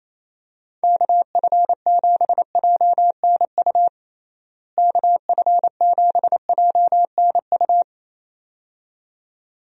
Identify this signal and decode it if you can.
Morse code, heard as a keyed tone, about 27 wpm, 700 Hz. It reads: KF7JNU KF7JNU
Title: USA Call Signs - 6 Characters - Repeated 2x - 28wpm